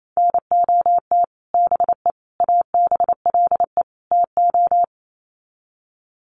Morse code, heard as a keyed tone, about 28 wpm, 700 Hz. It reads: NOT BE ABLE TO
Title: Sets of 4 Words 28wpm